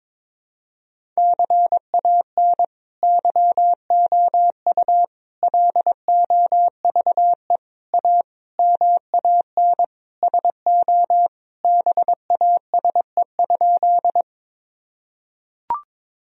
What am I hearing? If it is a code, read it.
Morse code, heard as a keyed tone, about 22 wpm, 700 Hz. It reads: CAN YOU LOVE A MAN SO BASE?
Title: Sentences from Top 1000 Words - Review Part 1 - 22wpm